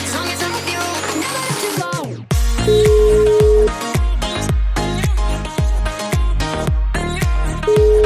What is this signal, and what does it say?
Telephone call progress tone (Europe, 425 Hz), heard as a telephone network's tone, about 420 Hz, with noise. Ringback tone.